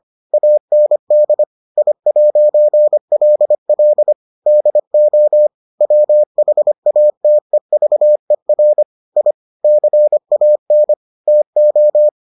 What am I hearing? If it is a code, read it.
Morse code, heard as a keyed tone, about 25 wpm, 600 Hz. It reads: AND I'LL DO WHATEVER I CAN TO